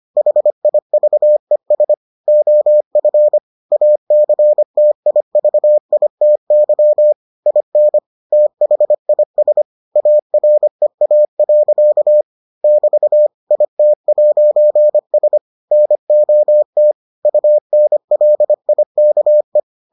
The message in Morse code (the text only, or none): HIVES OF ACTIVITY IN THIS AREA. <BT> IT'S NOT UNLIKE